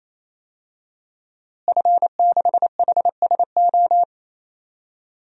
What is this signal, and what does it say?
Morse code, heard as a keyed tone, about 28 wpm, 700 Hz. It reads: F6HSO
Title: DXCC-100 Call Signs - 5 Characters - Repeated 3x - 28wpm